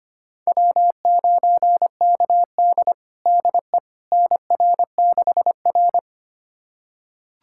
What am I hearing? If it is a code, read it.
Morse code, heard as a keyed tone, about 25 wpm, 700 Hz. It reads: W9KD DE NR6R
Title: USA Callsign DE Callsign - 4 Characters - 25wpm